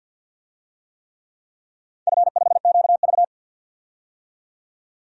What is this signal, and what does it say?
Morse code, heard as a keyed tone, about 51 wpm, 700 Hz. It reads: F5QV
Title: DXCC-100 Call Signs - 4 Characters - 50wpm